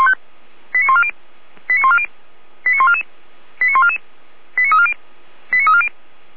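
Five-tone selective call (ZVEI), heard as quick runs of five tones, tones of about 70 ms. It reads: 89140 89140 89144 89144 89340 89340
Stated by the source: ZVEI five-tone selective call received with an SDR.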